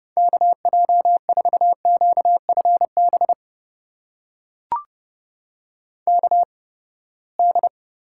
Morse code, heard as a keyed tone, about 32 wpm, spaced as for 21 wpm, 700 Hz. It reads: KJ4QFB KD
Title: USA Call Signs - 6 Characters - 8x Character Spacing - 30wpm